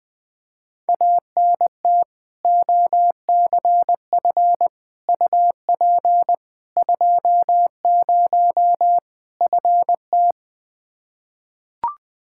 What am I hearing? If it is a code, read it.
Morse code, heard as a keyed tone, about 20 wpm, 700 Hz. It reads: ANT OCF UP 20 FT
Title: K1USN SST QSO Practice 20wpm